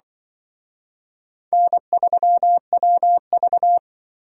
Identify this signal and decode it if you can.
Morse code, heard as a keyed tone, about 24 wpm, 700 Hz. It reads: N3WV
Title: Speed Racing - USA 1x2 Call Signs 20wpm